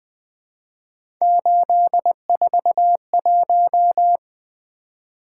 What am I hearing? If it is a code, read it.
Morse code, heard as a keyed tone, about 20 wpm, 700 Hz. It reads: 841
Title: Three Letter-Number Combinations - 20wpm